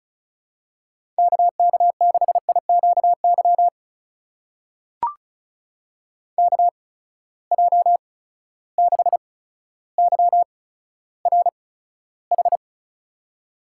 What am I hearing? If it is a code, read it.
Morse code, heard as a keyed tone, about 34 wpm, spaced as for 15 wpm, 700 Hz. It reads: KK6IQY KJ6YRH
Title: USA Call Signs - 6 Characters - 8x Character Spacing - 35wpm